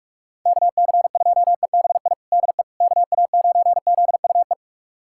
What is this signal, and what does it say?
Morse code, heard as a keyed tone, about 41 wpm, 700 Hz. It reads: KC2EBI DE KA9ZUE
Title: USA Callsign DE Callsign 45wpm